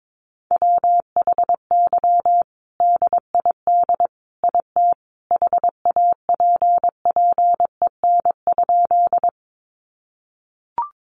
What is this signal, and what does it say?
Morse code, heard as a keyed tone, about 22 wpm, 700 Hz. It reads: WHY DID IT HAPPEN?
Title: Sentences from Top 400 Words 22wpm